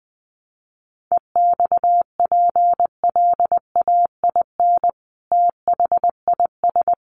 Morse code, heard as a keyed tone, about 20 wpm, 700 Hz. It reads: EXPLAIN THIS
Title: Sentences from Top 1200 Words 20wpm